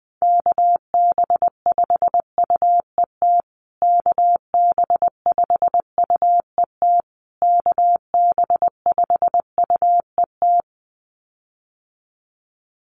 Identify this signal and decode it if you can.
Morse code, heard as a keyed tone, about 20 wpm, 700 Hz. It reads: KB5UET KB5UET KB5UET
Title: USA Call Signs Repeated 3x 20wpm